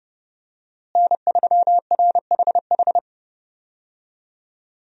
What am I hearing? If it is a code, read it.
Morse code, heard as a keyed tone, about 30 wpm, 700 Hz. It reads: N3RHH
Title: USA Call Signs - 5 Characters - Repeated 2x - 30wpm